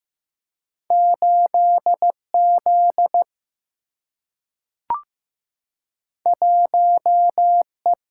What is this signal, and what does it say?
Morse code, heard as a keyed tone, about 15 wpm, 700 Hz. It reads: 8Z 1E
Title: Single-Letter Single-Number 15wpm